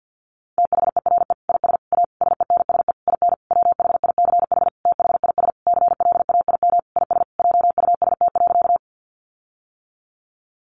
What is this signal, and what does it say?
Morse code, heard as a keyed tone, about 50 wpm, 700 Hz. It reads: THERE IS A SENSE IN WHICH THIS CLAIM IS JUST.
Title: Sentences from Top 1500 Words 50wpm